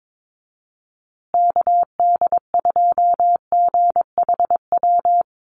Morse code, heard as a keyed tone, about 22 wpm, 700 Hz. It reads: KD2GHW
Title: USA Call Signs Repeated 3x 22wpm